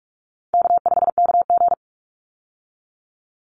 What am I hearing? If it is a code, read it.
Morse code, heard as a keyed tone, about 44 wpm, 700 Hz. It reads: K5KG
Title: Easy Call Signs Repeated 3x - 45wpm